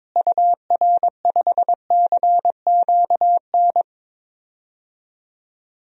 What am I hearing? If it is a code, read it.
Morse code, heard as a keyed tone, about 22 wpm, 700 Hz. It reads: UR5CQN
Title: DXCC-100 Call Signs - 6 Characters - Repeated 3x - 22wpm